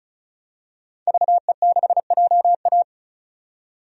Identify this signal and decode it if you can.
Morse code, heard as a keyed tone, about 34 wpm, 700 Hz. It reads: VE6JA
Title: DXCC-100 Call Signs - 5 Characters - 35wpm